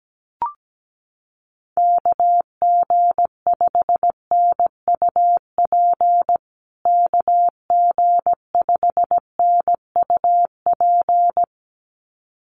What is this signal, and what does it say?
Morse code, heard as a keyed tone, about 17 wpm, 700 Hz. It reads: KG5NUP KG5NUP
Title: USA Call Signs Repeated 2x 17wpm